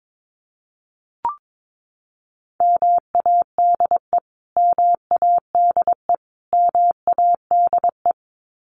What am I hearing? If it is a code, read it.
Morse code, heard as a keyed tone, about 22 wpm, 700 Hz. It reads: MADE MADE MADE
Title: Top 300 Words Repeated 22wpm